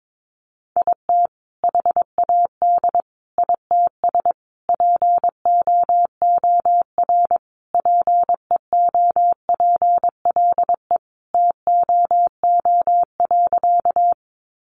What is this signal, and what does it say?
Morse code, heard as a keyed tone, about 22 wpm, 700 Hz. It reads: IT HAD ITS POOR PEOPLE TOO.
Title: Sentences from Top 1000 Words 22wpm